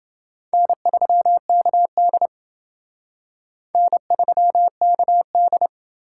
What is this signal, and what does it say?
Morse code, heard as a keyed tone, about 29 wpm, 700 Hz. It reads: N3KD N3KD